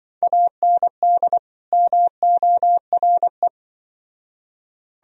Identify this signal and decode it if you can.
Morse code, heard as a keyed tone, about 24 wpm, 700 Hz. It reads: AND MORE
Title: Speed-Racing - Sets of 2 Words 20wpm